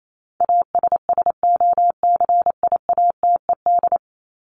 Morse code, heard as a keyed tone, about 28 wpm, 700 Hz. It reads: ASSOCIATED
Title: ED Suffix - Top 100 Words with Base Words 28wpm